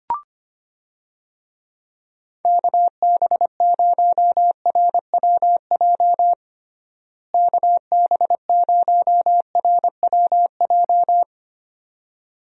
Morse code, heard as a keyed tone, about 25 wpm, 700 Hz. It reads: KB0RWJ KB0RWJ
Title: USA Call Signs - 6 Characters - Repeated 2x - 25wpm